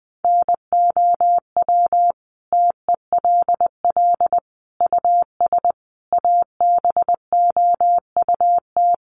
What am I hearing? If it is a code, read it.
Morse code, heard as a keyed tone, about 20 wpm, 700 Hz. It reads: NOW TELL US ABOUT
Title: Sentences from Top 600 Words 20wpm